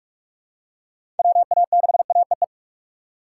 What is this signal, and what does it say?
Morse code, heard as a keyed tone, about 41 wpm, 700 Hz. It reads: WA6AEE
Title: K1USN SST QSO Practice - 45wpm